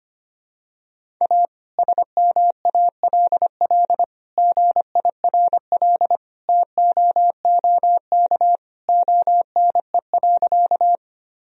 Morse code, heard as a keyed tone, about 25 wpm, 700 Hz. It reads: A SMALL GIRL TOOK ONE.
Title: Sentences from Top 1000 Words - Review Part 1 - 25wpm